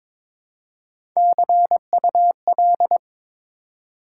Morse code, heard as a keyed tone, about 21 wpm, 700 Hz. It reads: CUL
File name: Top 90 QSO Elements - 3x Character Spacing - 22wpm